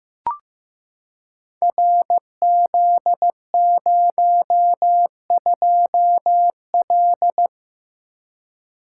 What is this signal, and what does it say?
Morse code, heard as a keyed tone, about 15 wpm, 700 Hz. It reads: RZ02L
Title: Code Groups - 5 Letter:Numbers - 15wpm